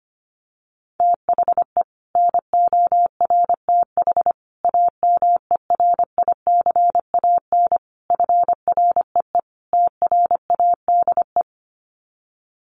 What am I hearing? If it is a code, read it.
Morse code, heard as a keyed tone, about 25 wpm, 700 Hz. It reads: THE NORTH AMERICAN FREE TRADE
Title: Sets of 5 Words Encore 25wpm